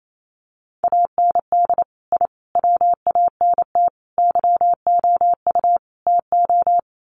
Morse code, heard as a keyed tone, about 28 wpm, 700 Hz. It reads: AND I WANT YOU TO
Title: Sets of 5 Words Encore 28wpm